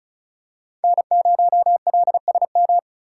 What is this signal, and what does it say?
Morse code, heard as a keyed tone, about 34 wpm, 690 Hz. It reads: N0LSM
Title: USA Call Signs - 5 Characters - 3x Character Spacing - 35wpm